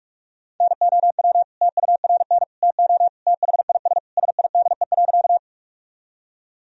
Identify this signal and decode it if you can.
Morse code, heard as a keyed tone, about 41 wpm, 700 Hz. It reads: NOW TURN TO THIS SIDE.
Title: Sentences from Top 1000 Words - Review Part 1 - 45wpm